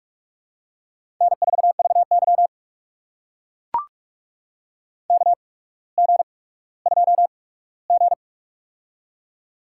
Morse code, heard as a keyed tone, about 44 wpm, spaced as for 19 wpm, 700 Hz. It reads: N4VY KG2G
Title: USA Call Signs - 4 Characters - 8x Character Spacing - 45wpm